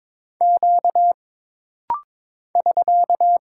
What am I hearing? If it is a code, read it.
Morse code, heard as a keyed tone, about 22 wpm, 700 Hz. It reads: Q <SK>
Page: This rapid-fire timing is geared to push your ICR (Instant Character Recognition) skills.